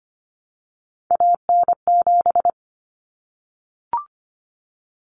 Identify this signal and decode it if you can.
Morse code, heard as a keyed tone, about 25 wpm, 700 Hz. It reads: AN7
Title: Common Cut Numbers - 3-Digit 25wpm